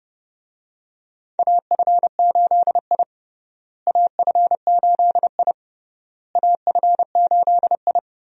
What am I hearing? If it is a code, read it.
Morse code, heard as a keyed tone, about 30 wpm, 700 Hz. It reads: AF8I AF8I AF8I
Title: USA Call Signs - 4 Characters - Repeated 3x - 30wpm